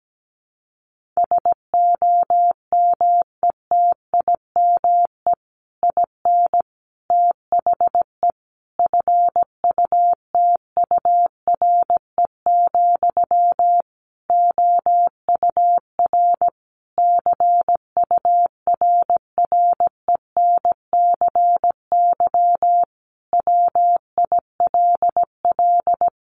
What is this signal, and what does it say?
Morse code, heard as a keyed tone, about 17 wpm, 700 Hz. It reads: SOMETIME IN THE FUTURE, OUR CURRENCY WILL
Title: Sentences from Top 1500 Words 17wpm